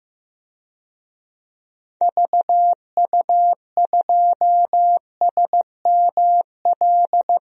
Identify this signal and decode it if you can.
Morse code, heard as a keyed tone, about 16 wpm, 700 Hz. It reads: VU2SML
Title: DXCC-100 Call Signs - 6 Characters - 15wpm